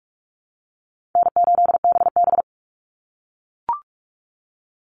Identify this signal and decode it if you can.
Morse code, heard as a keyed tone, about 44 wpm, 700 Hz. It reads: N8BB
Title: NA SOTA Chaser Callsigns 45wpm